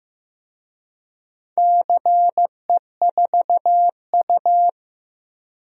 Morse code, heard as a keyed tone, about 15 wpm, 700 Hz. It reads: CE4U
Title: DXCC-100 Call Signs - 4 Characters - Repeated 2x - 15wpm